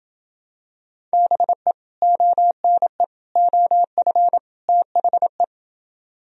Morse code, heard as a keyed tone, about 28 wpm, 700 Hz. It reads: BE ONE OF THE